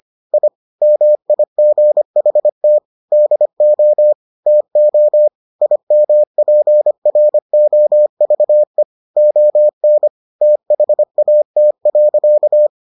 Morse code, heard as a keyed tone, about 25 wpm, 600 Hz. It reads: I MIGHT DO TO IMPROVE ON THAT.